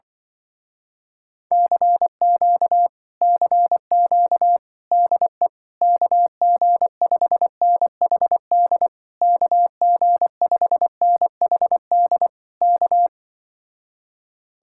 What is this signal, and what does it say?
Morse code, heard as a keyed tone, about 25 wpm, 700 Hz. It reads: CQ CQ DE KG5NHD KG5NHD K